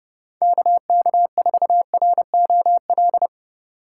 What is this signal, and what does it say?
Morse code, heard as a keyed tone, about 31 wpm, 700 Hz. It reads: KK4ROL
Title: Speed Racing - USA 2x3 Call Signs 25wpm